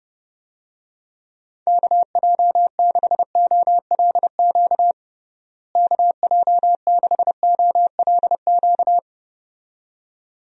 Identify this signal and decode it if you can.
Morse code, heard as a keyed tone, about 30 wpm, 700 Hz. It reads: KJ6OLQ KJ6OLQ
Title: USA Call Signs - 6 Characters - Repeated 2x - 30wpm